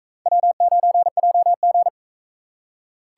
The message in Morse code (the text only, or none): W9JG